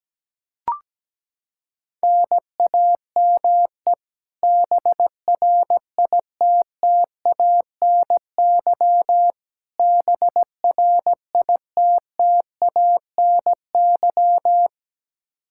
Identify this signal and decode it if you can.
Morse code, heard as a keyed tone, about 17 wpm, 700 Hz. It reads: NAME BRITTANY BRITTANY
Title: Practice QSO - Exchange 1 - Beginning - 17wpm